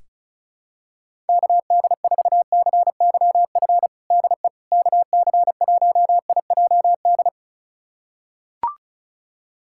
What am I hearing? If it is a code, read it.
Morse code, heard as a keyed tone, about 36 wpm, 700 Hz. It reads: KD4CYF DE KC1IJD
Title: USA Callsign DE Callsign - 6 Characters - 35wpm